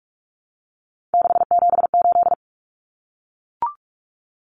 Morse code, heard as a keyed tone, about 44 wpm, 700 Hz. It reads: 678
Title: Common Cut Numbers - 3-Digit 45wpm